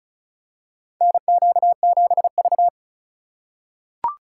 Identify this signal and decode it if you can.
Morse code, heard as a keyed tone, about 36 wpm, 700 Hz. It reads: NQ7V
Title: USA Call Signs - 4 Characters - 4x Character Spacing - 35wpm